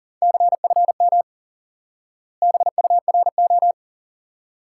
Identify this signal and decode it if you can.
Morse code, heard as a keyed tone, about 37 wpm, 700 Hz. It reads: CFM BURO
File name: IQE Course 3 - Lesson 89 - YAGI - Two Elements - 40wpm